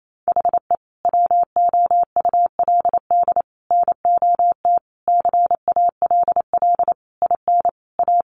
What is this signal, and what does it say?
Morse code, heard as a keyed tone, about 28 wpm, 700 Hz. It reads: HE WOULD NOT CALL IN A
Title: Sentences from Top 1500 Words 28wpm